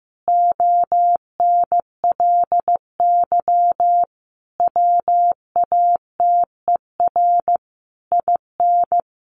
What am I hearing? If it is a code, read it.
Morse code, heard as a keyed tone, about 15 wpm, 700 Hz. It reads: ONLY WATER IN
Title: Sentences from Top 1500 Words 15wpm